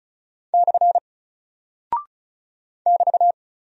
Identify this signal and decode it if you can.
Morse code, heard as a keyed tone, about 36 wpm, 700 Hz. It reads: / <BT>
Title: Single Character (Letter:Number:Punctuation:Prosign) 35wpm